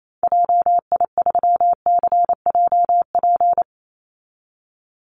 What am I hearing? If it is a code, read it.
Morse code, heard as a keyed tone, about 28 wpm, 700 Hz. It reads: JI3CJP
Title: Call Signs Repeated 2x - 28wpm